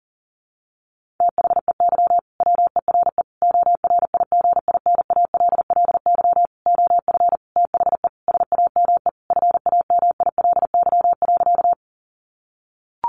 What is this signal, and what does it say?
Morse code, heard as a keyed tone, about 40 wpm, 700 Hz. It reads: THEY WERE ORIGINALLY OF THE SAME FAMILY.
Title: Sentences from Top 1500 Words 40wpm